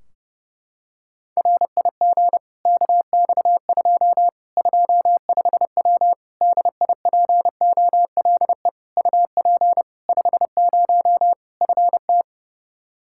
Morse code, heard as a keyed tone, about 30 wpm, 700 Hz. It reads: RIG KX2 25W DIPOLE UP 50 FT
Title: Practice QSO - Exchange 2 - Info - 30wpm